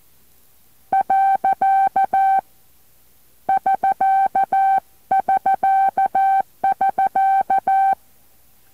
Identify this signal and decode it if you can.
Morse code, heard as a keyed tone, about 14 wpm, 780 Hz. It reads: . <SK><SK><SK>